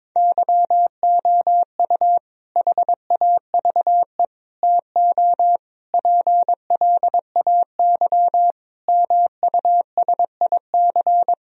Morse code, heard as a keyed tone, about 22 wpm, 700 Hz. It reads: YOU HAVE TO PLAY MUSIC
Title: Sentences from Top 1000 Words - Review Part 1 - 22wpm